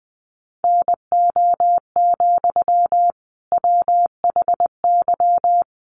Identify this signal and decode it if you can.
Morse code, heard as a keyed tone, about 20 wpm, 700 Hz. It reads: NO, WHY
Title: Sentences from Top 300 Words 20wpm